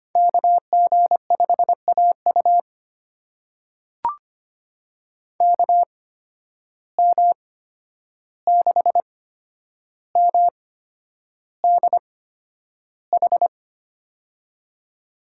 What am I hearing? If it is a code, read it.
Morse code, heard as a keyed tone, about 25 wpm, spaced as for 10 wpm, 700 Hz. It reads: KG5AU KM6MDH
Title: USA Call Signs - 8x Character Spacing - 25wpm